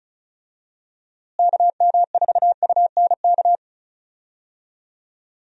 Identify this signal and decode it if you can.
Morse code, heard as a keyed tone, about 34 wpm, 700 Hz. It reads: KM4UNK
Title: USA Call Signs - 6 Characters - Repeated 2x - 35wpm